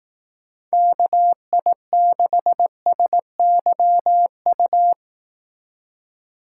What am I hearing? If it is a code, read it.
Morse code, heard as a keyed tone, about 18 wpm, 700 Hz. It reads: KI6SYU
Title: Speed Racing - USA 2x3 Call Signs 15wpm